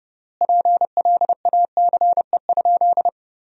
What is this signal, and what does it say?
Morse code, heard as a keyed tone, about 30 wpm, 700 Hz. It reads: PLACE?
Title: 4x Word-Spacing - Sentences from Top 200 Words - 30wpm